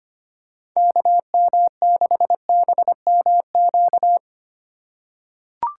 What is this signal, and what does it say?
Morse code, heard as a keyed tone, about 25 wpm, 700 Hz. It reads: KM6BMQ
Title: USA Call Signs - 3x Character Spacing - 25wpm